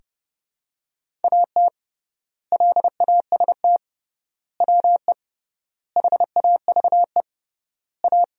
Morse code, heard as a keyed tone, about 30 wpm, 700 Hz. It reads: AT LAST WE HAVE A
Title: 3x Word-Spacing - Sentences from Top 300 Words - 30wpm